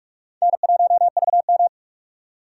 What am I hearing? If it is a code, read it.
Morse code, heard as a keyed tone, about 43 wpm, 700 Hz. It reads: N1VM
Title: USA Call Signs - 4 Characters - 2x Character Spacing - 45wpm